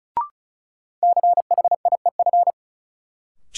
Morse code, heard as a keyed tone, about 36 wpm, 700 Hz. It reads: CHIEF
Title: Sets of 5 Words Encore - New Word Drill - 35wpm